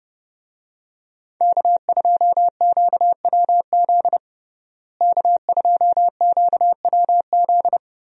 Morse code, heard as a keyed tone, about 32 wpm, 700 Hz. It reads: K2QWZ K2QWZ
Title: USA Call Signs - 5 Characters - Repeated 2x - 30wpm